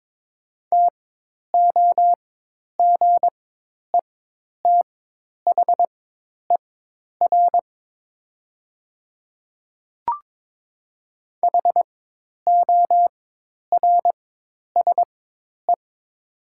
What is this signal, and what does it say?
Morse code, heard as a keyed tone, about 22 wpm, spaced as for 10 wpm, 700 Hz. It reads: TOGETHER HORSE
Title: 4x Character Spacing - Top 500 Words - 22wpm